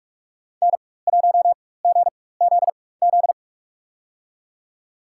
Morse code, heard as a keyed tone, about 43 wpm, spaced as for 21 wpm, 700 Hz. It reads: N1GZZ
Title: USA Call Signs - 5 Characters - 4x Character Spacing - 45wpm